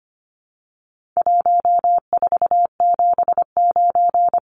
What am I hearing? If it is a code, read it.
Morse code, heard as a keyed tone, about 25 wpm, 700 Hz. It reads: 1479
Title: Numbers - Contest Style - 4-Digit 25wpm